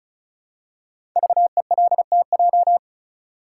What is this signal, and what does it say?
Morse code, heard as a keyed tone, about 36 wpm, 700 Hz. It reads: VELTJ
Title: Code Groups - 5 Letter:Numbers - 35wpm